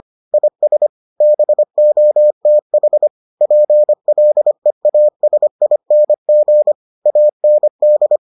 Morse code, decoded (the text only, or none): IS BOTH PLEASING AND